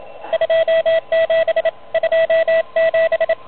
Morse code, heard as a keyed tone, about 32 wpm, 590 Hz, with radio noise. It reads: 27 27
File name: truSDX / MWC29.8